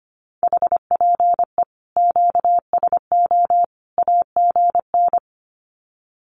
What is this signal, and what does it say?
Morse code, heard as a keyed tone, about 25 wpm, 700 Hz. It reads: HPE QSO AGN
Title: Common QSO Phrases 25wpm